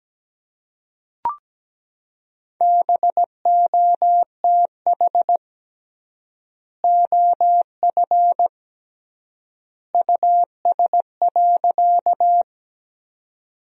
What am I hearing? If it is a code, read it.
Morse code, heard as a keyed tone, about 17 wpm, 700 Hz. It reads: BOTH OF US.
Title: 3x Word-Spacing - Sentences from Top 300 Words - 17wpm